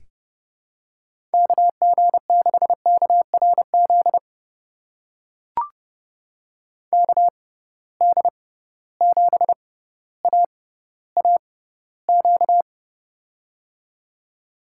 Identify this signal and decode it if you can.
Morse code, heard as a keyed tone, about 32 wpm, spaced as for 16 wpm, 700 Hz. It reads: KG6KRZ KD7AAQ
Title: USA Call Signs - 6 Characters - 6x Character Spacing - 30wpm